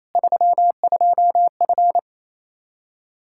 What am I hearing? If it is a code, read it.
Morse code, heard as a keyed tone, about 27 wpm, 700 Hz. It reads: 32F
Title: Practice QSO - Exchange 2 - Info - 28wpm